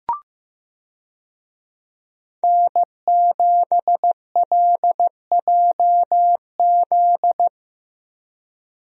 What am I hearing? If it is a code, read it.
Morse code, heard as a keyed tone, about 15 wpm, 700 Hz. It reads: N7LJZ
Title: USA Call Signs - 5 Characters - 15wpm